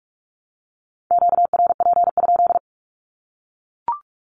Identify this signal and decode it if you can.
Morse code, heard as a keyed tone, about 44 wpm, 700 Hz. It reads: QRP?
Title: Q-Codes 45wpm